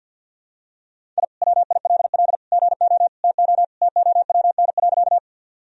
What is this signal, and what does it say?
Morse code, heard as a keyed tone, about 48 wpm, 700 Hz. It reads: I WILL GO TO TOWN.
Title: Sentences from Top 1000 Words - Review Part 2 - 50wpm